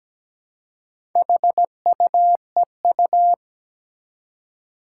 Morse code, heard as a keyed tone, about 17 wpm, 700 Hz. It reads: HUEU
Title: Four Challenging Letter-Number Combinations 17wpm